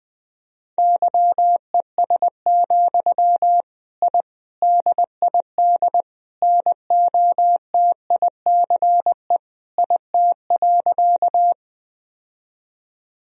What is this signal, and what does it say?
Morse code, heard as a keyed tone, about 20 wpm, 700 Hz. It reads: YES, I DID NOTICE IT.
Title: Sentences from Top 1000 Words - Review Part 2 - 20wpm